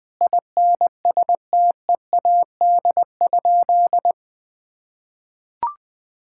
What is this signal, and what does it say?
Morse code, heard as a keyed tone, about 21 wpm, 700 Hz. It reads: INSTEAD?
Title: Sentences from Top 1000 Words - Review Part 6 - 20wpm